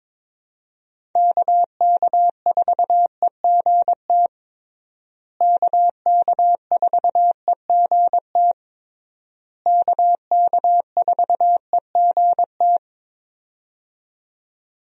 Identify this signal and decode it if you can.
Morse code, heard as a keyed tone, about 22 wpm, 700 Hz. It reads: KK4EGT KK4EGT KK4EGT
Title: USA Call Signs - 6 Characters - Repeated 3x - 22wpm